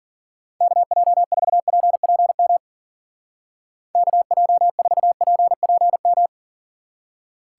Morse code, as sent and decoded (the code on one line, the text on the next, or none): -.- .--- ....- .--. .--. -- / -.- .--- ....- .--. .--. --
KJ4PPM KJ4PPM